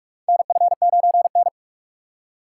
Morse code, heard as a keyed tone, about 46 wpm, 700 Hz. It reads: NF9N
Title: USA Call Signs - 4 Characters - 6x Character Spacing - 45wpm